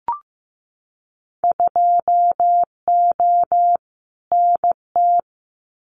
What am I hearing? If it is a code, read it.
Morse code, heard as a keyed tone, about 15 wpm, 700 Hz. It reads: 2O NT
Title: Winter Field Day Exchanges Encore 15wpm